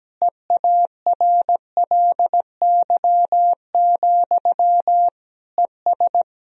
Morse code, heard as a keyed tone, about 17 wpm, 700 Hz. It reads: EARLY, ES
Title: Sentences from Top 400 Words and Abbreviations 17wpm